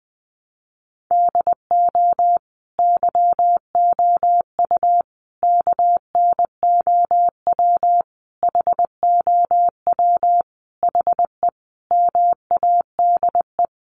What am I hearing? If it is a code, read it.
Morse code, heard as a keyed tone, about 20 wpm, 700 Hz. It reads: DO YOU KNOW HOW HE MADE
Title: Sentences from Top 300 Words 20wpm